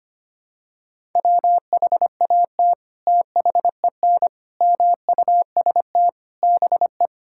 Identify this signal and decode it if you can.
Morse code, heard as a keyed tone, about 25 wpm, 700 Hz. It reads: WHAT THEN MUST BE
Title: Sentences from Top 1000 Words - Review Part 5 - 25wpm